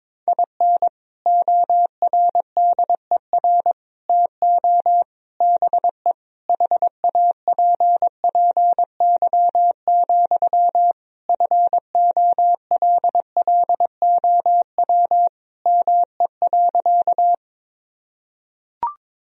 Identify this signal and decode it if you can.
Morse code, heard as a keyed tone, about 22 wpm, 700 Hz. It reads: IN ORDER TO BE HAPPY, FOLLOW ME.
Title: Sentences from Top 1000 Words - Review Part 2 - 22wpm